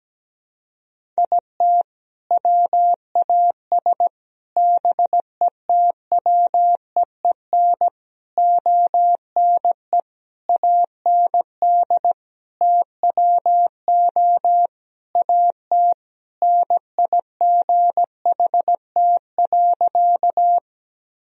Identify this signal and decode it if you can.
Morse code, heard as a keyed tone, about 17 wpm, 700 Hz. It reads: IT WAS BETWEEN ONE AND TWO AT NIGHT.
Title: Sentences from Top 1000 Words - Review Part 1 - 17wpm